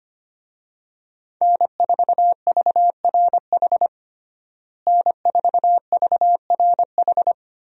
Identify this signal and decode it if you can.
Morse code, heard as a keyed tone, about 25 wpm, 700 Hz. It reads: N4VRH N4VRH
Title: USA Call Signs - 5 Characters - Repeated 2x - 25wpm